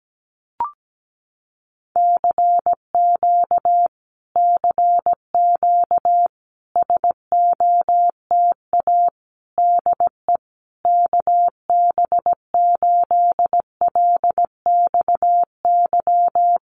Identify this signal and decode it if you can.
Morse code, heard as a keyed tone, about 17 wpm, 700 Hz. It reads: CQ CQ SOTA DE KB8LXY
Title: Common QSO Phrases Encore 17wpm